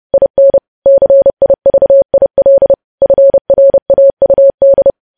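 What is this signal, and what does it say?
Morse code, heard as a keyed tone, about 30 wpm, 550 Hz. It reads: IN CIVIL FRAUD
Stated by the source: News headlines sent in Morse at 30 wpm, preceded by VVV.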